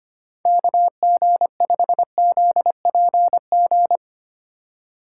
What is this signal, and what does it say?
Morse code, heard as a keyed tone, about 32 wpm, 700 Hz. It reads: KG5ZPG